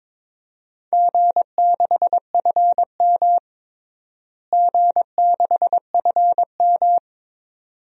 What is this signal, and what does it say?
Morse code, heard as a keyed tone, about 22 wpm, 700 Hz. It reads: G6FM G6FM
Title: DXCC-100 Call Signs - 4 Characters - Repeated 2x - 22wpm